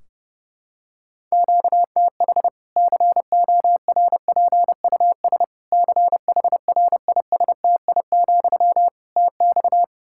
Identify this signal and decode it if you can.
Morse code, heard as a keyed tone, about 30 wpm, 700 Hz. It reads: QTH CORPUS CHRISTI, TX
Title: Practice QSO - Exchange 1 - Beginning - 30wpm